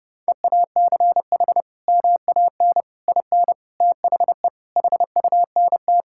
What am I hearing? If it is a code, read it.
Morse code, heard as a keyed tone, about 32 wpm, 700 Hz. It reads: EACH MAN IN THE HUNT
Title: Sentences from Top 1600 Words 30wpm